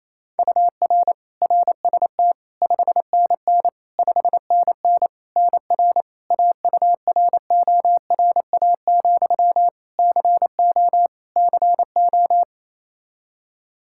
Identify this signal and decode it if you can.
Morse code, heard as a keyed tone, about 28 wpm, 700 Hz. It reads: UR RST 5NN 5NN NR AURORA, CO CO
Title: Practice QSO - Exchange 1 - Beginning - 28wpm